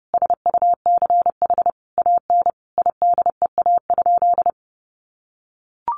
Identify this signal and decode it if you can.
Morse code, heard as a keyed tone, about 30 wpm, 700 Hz. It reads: SUCH AN IDEA?
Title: Sentences from Top 400 Words 30wpm